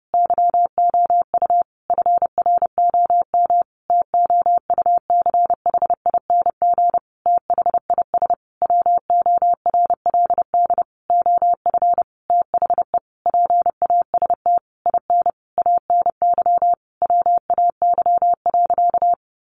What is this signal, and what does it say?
Morse code, heard as a keyed tone, about 30 wpm, 700 Hz. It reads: YOU FROM TOUCHING THIS WORLD OF THE PAST IN ANY WAY.
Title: A Sound Of Thunder 30wpm